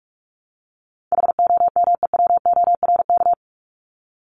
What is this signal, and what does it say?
Morse code, heard as a keyed tone, about 44 wpm, 700 Hz. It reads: HOMEWORK
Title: Top 1501-1750 Words 45wpm